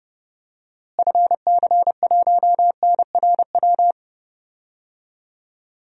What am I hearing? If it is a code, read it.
Morse code, heard as a keyed tone, about 30 wpm, 700 Hz. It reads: FC1NRW
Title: DXCC-100 Call Signs - 6 Characters - Repeated 3x - 30wpm